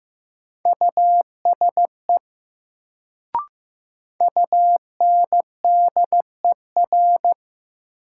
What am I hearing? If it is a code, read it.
Morse code, heard as a keyed tone, about 15 wpm, 700 Hz. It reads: USE UNDER
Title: Sets of 2 Words - New Word Drill - 15wpm